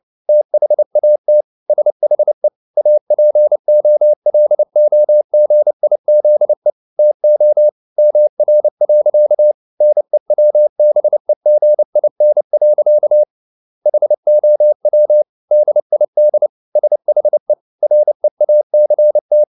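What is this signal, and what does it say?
Morse code, heard as a keyed tone, about 29 wpm, 600 Hz. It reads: THAT SHE APOLOGIZE TO MR. NEWBEGIN. HOW DID SHE REACT
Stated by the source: Morse code at 29 wpm, sending Ken Follett's novel Never in English, from CW med Gnister.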